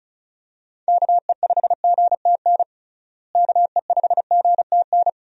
Morse code, heard as a keyed tone, about 36 wpm, 700 Hz. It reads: KE5GTN KE5GTN
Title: USA Call Signs - 6 Characters - Repeated 2x - 35wpm